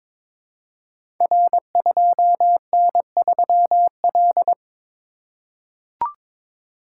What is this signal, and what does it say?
Morse code, heard as a keyed tone, about 22 wpm, 700 Hz. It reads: R2N3L
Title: Code Groups - 5 Letter:Numbers - 22wpm